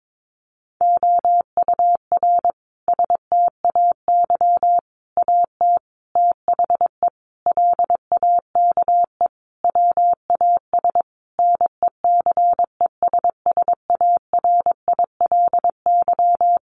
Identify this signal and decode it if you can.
Morse code, heard as a keyed tone, about 22 wpm, 700 Hz. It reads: OUR STAY AT THE LAKE WAS NECESSARILY
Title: Sentences from Top 1400 Words 22wpm